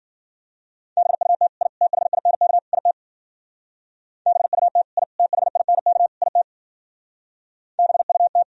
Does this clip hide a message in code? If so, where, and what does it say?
Yes: Morse code from 1.0 to 8.4 s, heard as a keyed tone, about 51 wpm, 700 Hz. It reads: BUT I THINK IT BUT I THINK IT BUT